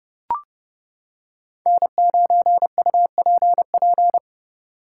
Morse code, heard as a keyed tone, about 29 wpm, 700 Hz. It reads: N9UPP